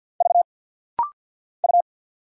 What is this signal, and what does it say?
Morse code, heard as a keyed tone, about 50 wpm, 690 Hz. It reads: V U
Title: Single Letters - Rapid-Fire 50wpm